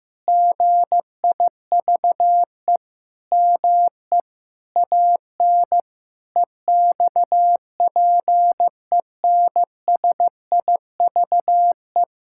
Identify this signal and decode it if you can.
Morse code, heard as a keyed tone, about 15 wpm, 700 Hz. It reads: GIVE ME AN EXPENSIVE
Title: Sentences from Top 1000 Words - Review Part 6 - 15wpm